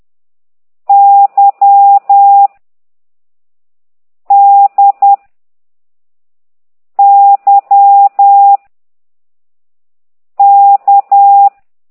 Morse code, heard as a keyed tone, about 10 wpm, 800 Hz. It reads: Y D Y K
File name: ct2idw / exerc4